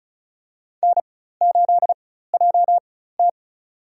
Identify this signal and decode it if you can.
Morse code, heard as a keyed tone, about 34 wpm, spaced as for 16 wpm, 700 Hz. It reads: N8JT
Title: USA Call Signs - 4 Characters - 4x Character Spacing - 35wpm